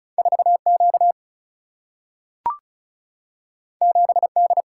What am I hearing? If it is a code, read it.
Morse code, heard as a keyed tone, about 36 wpm, 700 Hz. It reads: 4 Q 7 D
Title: Single-Letter Single-Number 35wpm